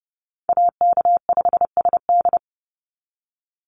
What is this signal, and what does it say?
Morse code, heard as a keyed tone, about 32 wpm, 700 Hz. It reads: AK5SD
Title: NA SOTA Chaser Callsigns 30wpm